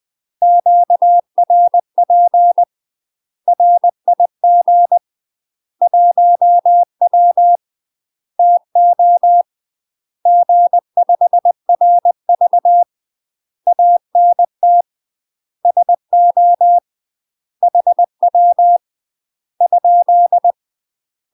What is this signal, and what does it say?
Morse code, heard as a keyed tone, about 20 wpm, 700 Hz. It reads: QRP RIG 1W TO G5RV ANT SO HW ?
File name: UK MASTER QSOs TRAINING / 20WPM / UK MASTER QSOs 0036